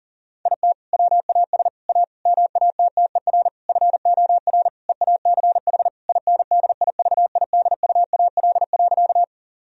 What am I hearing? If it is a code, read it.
Morse code, heard as a keyed tone, about 40 wpm, 700 Hz. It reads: IT WAS A MATTER FOR EACH INDIVIDUAL.
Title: Sentences from Top 1000 Words - Review Part 4 - 40wpm